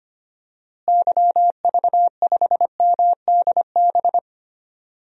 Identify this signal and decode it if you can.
Morse code, heard as a keyed tone, about 25 wpm, 700 Hz. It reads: YV5MDB
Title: DXCC-100 Call Signs - 6 Characters - Repeated 2x - 25wpm